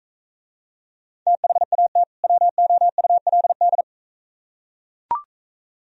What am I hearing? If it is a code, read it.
Morse code, heard as a keyed tone, about 43 wpm, 700 Hz. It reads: THAT WOULD